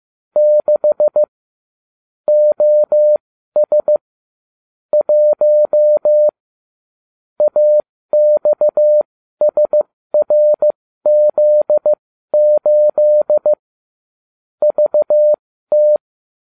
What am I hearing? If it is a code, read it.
Morse code, heard as a keyed tone, about 15 wpm, spaced as for 12 wpm, 600 Hz. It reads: 6 OS 1 AXSRZ8 VT